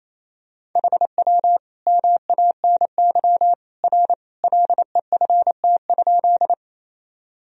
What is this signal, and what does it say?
Morse code, heard as a keyed tone, about 27 wpm, 700 Hz. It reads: HW MANY R LEFT?
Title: Sentences from Top 300 Words and Abbreviations 28wpm